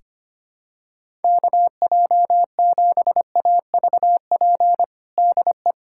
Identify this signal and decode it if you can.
Morse code, heard as a keyed tone, about 25 wpm, 700 Hz. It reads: KJ7AVP DE
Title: USA Callsign DE Callsign 25wpm